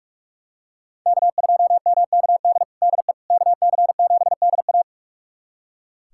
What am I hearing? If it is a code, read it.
Morse code, heard as a keyed tone, about 42 wpm, 700 Hz. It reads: K2MKD DE KC7DA
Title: USA Callsign DE Callsign - 5 Characters - 45wpm